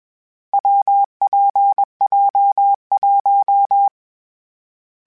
Morse code, heard as a keyed tone, about 21 wpm, 800 Hz. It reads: WPJ1
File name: Difficult Combos / Difficult Combos-mp3s / WPJ WPJ1 @20